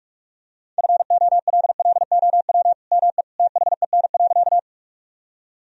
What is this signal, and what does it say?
Morse code, heard as a keyed tone, about 46 wpm, 700 Hz. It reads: FOLLOW ME THEN.
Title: Sentences from Top 1000 Words - Review Part 1 - 45wpm